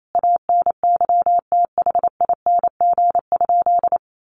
Morse code, heard as a keyed tone, about 28 wpm, 700 Hz. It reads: ANYTHING?
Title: Sentences from Top 1000 Words 28wpm